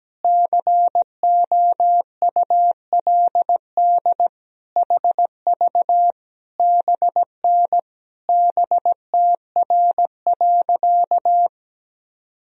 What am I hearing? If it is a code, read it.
Morse code, heard as a keyed tone, about 17 wpm, 700 Hz. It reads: COULD HV BN BTR.
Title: Sentences from Top 500 Words and Abbreviations 17wpm